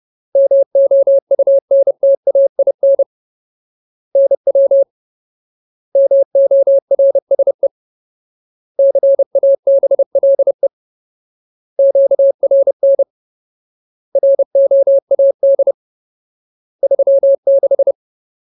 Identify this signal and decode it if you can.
Morse code, heard as a keyed tone, about 30 wpm, 550 Hz. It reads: MOUNTAIN NW MORSE CABLE QRN ROAD 36
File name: Ultimate Word List / 30WPM Ultimate